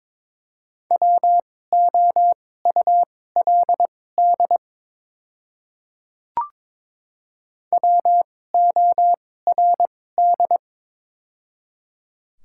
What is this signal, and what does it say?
Morse code, heard as a keyed tone, about 22 wpm, spaced as for 16 wpm, 700 Hz. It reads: WOULD WORD
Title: 2x Character Spacing - Top 100 Words - 22wpm